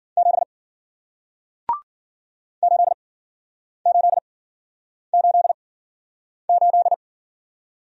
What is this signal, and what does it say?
Morse code, heard as a keyed tone, about 51 wpm, spaced as for 9 wpm, 700 Hz. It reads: 7 8888